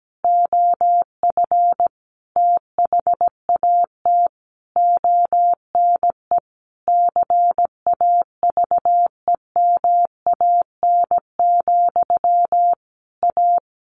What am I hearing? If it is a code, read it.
Morse code, heard as a keyed tone, about 17 wpm, 700 Hz. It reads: OF THAT ONE CAVEMAN, A
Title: A Sound Of Thunder 17wpm